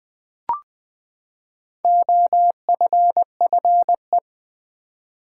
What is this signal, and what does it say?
Morse code, heard as a keyed tone, about 20 wpm, 700 Hz. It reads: OFFE
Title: Build A Word - 4 Letters At A Time - 8-Letter Words - 20wpm